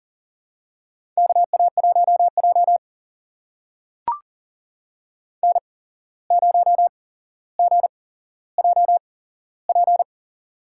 Morse code, heard as a keyed tone, about 40 wpm, spaced as for 16 wpm, 700 Hz. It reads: KA1J N0GJP
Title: USA Call Signs - 8x Character Spacing - 40wpm